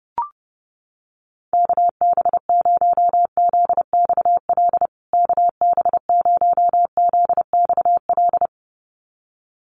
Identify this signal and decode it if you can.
Morse code, heard as a keyed tone, about 30 wpm, 700 Hz. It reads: KB0ZXL KB0ZXL
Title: USA Call Signs Repeated 2x 30wpm